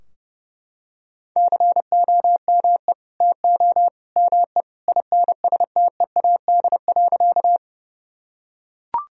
Morse code, heard as a keyed tone, about 30 wpm, 700 Hz. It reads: COME TO ME INSTEAD.
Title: Sentences from Top 1000 Words - Review Part 4 - 30wpm